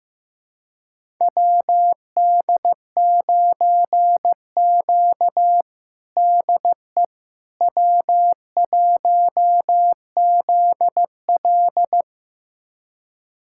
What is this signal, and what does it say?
Morse code, heard as a keyed tone, about 16 wpm, 700 Hz. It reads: WD9Q DE W1ZL
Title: USA Callsign DE Callsign - 4 Characters - 15wpm